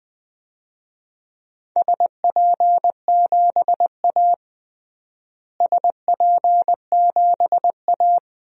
Morse code, heard as a keyed tone, about 20 wpm, 700 Hz. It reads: SP7A SP7A
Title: DXCC-100 Call Signs - 4 Characters - Repeated 2x - 20wpm